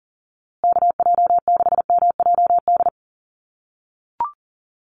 Morse code, heard as a keyed tone, about 40 wpm, 700 Hz. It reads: KJ6MJD
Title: USA Call Signs Repeated 3x 40wpm